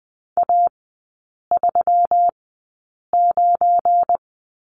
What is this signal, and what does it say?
Morse code, heard as a keyed tone, about 20 wpm, spaced as for 8 wpm, 690 Hz. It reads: A39
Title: 198 Lesson 32e - Letter k - Three Characters 20wpm